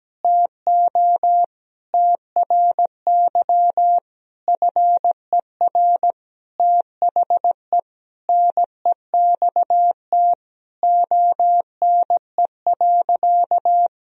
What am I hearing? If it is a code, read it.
Morse code, heard as a keyed tone, about 17 wpm, 700 Hz. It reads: TO TRY FER THE NEXT ONE.
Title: Sentences from Top 400 Words and Abbreviations 17wpm